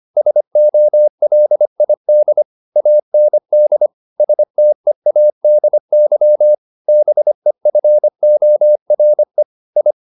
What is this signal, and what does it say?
Morse code, heard as a keyed tone, about 25 wpm, 600 Hz. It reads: SOLID AND STEADY BEFORE I